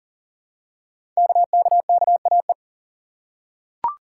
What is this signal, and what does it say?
Morse code, heard as a keyed tone, about 39 wpm, 700 Hz. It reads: KKKAE
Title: Code Groups - 5 Letters - 40wpm